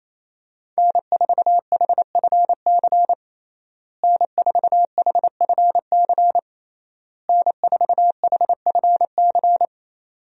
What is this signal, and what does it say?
Morse code, heard as a keyed tone, about 28 wpm, 700 Hz. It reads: N4HFC N4HFC N4HFC
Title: USA Call Signs - 5 Characters - Repeated 3x - 28wpm